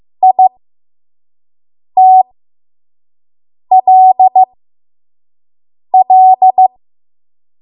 Morse code, heard as a keyed tone, about 15 wpm, spaced as for 5 wpm, 750 Hz. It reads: ITLL
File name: Morse / 05WPM